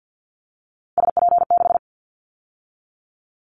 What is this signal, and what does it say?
Morse code, heard as a keyed tone, about 50 wpm, 700 Hz. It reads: SP6
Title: Callsign Prefixes for Top 25 DXCC - 50wpm